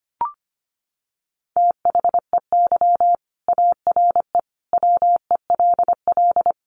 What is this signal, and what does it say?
Morse code, heard as a keyed tone, about 25 wpm, 700 Hz. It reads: THEY ARE WELL
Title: Sentences from Top 1300 Words 25wpm